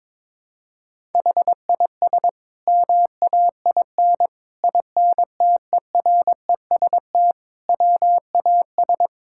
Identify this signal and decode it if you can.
Morse code, heard as a keyed tone, about 21 wpm, 700 Hz. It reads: HIS MAIN INTEREST WAS
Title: Sentences from Top 1000 Words - Review Part 6 - 22wpm